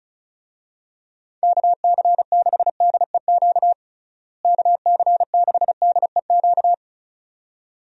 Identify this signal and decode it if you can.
Morse code, heard as a keyed tone, about 36 wpm, 700 Hz. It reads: KC6DEQ KC6DEQ
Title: USA Call Signs - 6 Characters - Repeated 2x - 35wpm